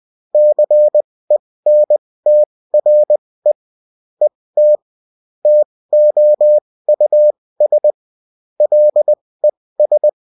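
Morse code, heard as a keyed tone, about 20 wpm, spaced as for 16 wpm, 600 Hz. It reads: CENTRE ET TOUS LES
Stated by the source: UFT bulletin sent in CW.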